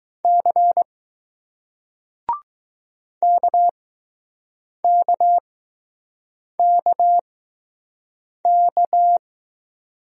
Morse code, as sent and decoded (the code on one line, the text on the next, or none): -.-. / -.- -.- -.- -.-
C KKKK